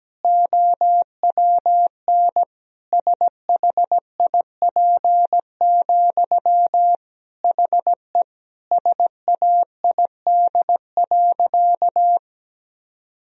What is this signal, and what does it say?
Morse code, heard as a keyed tone, about 17 wpm, 700 Hz. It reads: OWN SHIP, HE SAID.
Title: Sentences from Top 1000 Words - Review Part 2 - 17wpm